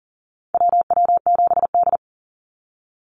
Morse code, heard as a keyed tone, about 40 wpm, 700 Hz. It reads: WW7D
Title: NA SOTA Activator Callsigns 40wpm